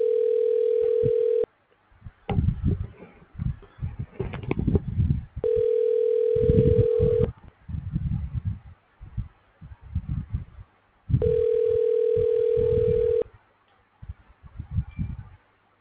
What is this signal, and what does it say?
Ringback tone (North America), 440 + 480 Hz.